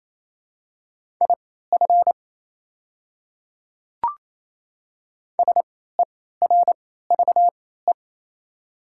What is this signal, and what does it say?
Morse code, heard as a keyed tone, about 28 wpm, spaced as for 16 wpm, 700 Hz. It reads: IF SERVE
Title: 3x Character Spacing - Top 500 Words - 28wpm